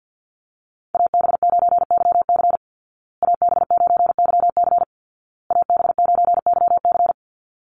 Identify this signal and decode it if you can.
Morse code, heard as a keyed tone, about 48 wpm, 700 Hz. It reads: AB9YC AB9YC AB9YC
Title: Call Signs Repeated 2x - 50wpm